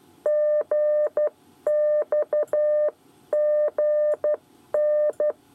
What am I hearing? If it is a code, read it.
Morse code, heard as a keyed tone, about 12 wpm, 600 Hz. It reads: GXGN
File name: 8 bis lezione cw 1